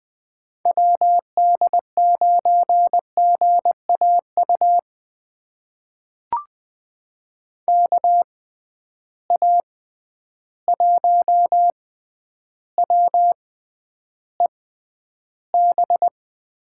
Morse code, heard as a keyed tone, about 20 wpm, spaced as for 10 wpm, 700 Hz. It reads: WD9GAU KA1WEB
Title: USA Call Signs - 6 Characters - 6x Character Spacing - 20wpm